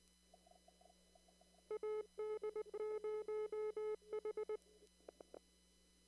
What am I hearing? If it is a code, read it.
Morse code, heard as a keyed tone, about 19 wpm, 1280 Hz. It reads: AD0H
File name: CW / kt5j wpx cw-op-k5pi